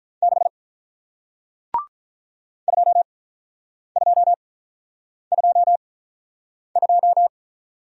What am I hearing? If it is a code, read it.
Morse code, heard as a keyed tone, about 48 wpm, spaced as for 9 wpm, 700 Hz. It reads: 6 2222